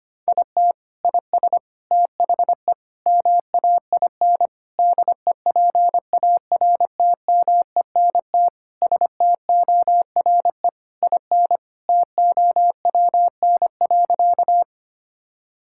Morse code, heard as a keyed tone, about 25 wpm, 700 Hz. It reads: IT IS THE MAIN DEPARTMENT STORE IN TOWN.
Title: Sentences from Top 1000 Words - Review Part 5 - 25wpm